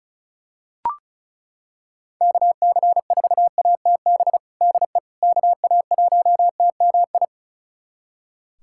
Morse code, heard as a keyed tone, about 36 wpm, 700 Hz. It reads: KC4ATB DE KA1TMI
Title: USA Callsign DE Callsign - 6 Characters - 35wpm